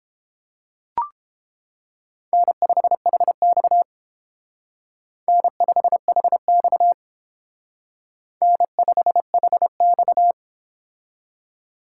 Morse code, heard as a keyed tone, about 31 wpm, 700 Hz. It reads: N5HX N5HX N5HX